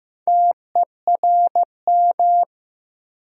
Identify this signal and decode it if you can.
Morse code, heard as a keyed tone, about 15 wpm, 700 Hz. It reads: TERM
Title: Sentences from Top 900 Words - New Word Drill - 15wpm